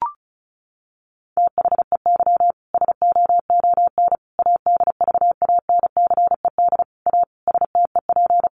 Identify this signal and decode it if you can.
Morse code, heard as a keyed tone, about 34 wpm, 700 Hz. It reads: THEY SOON ADVANCED A STEP
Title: Sentences from Top 1200 Words 35wpm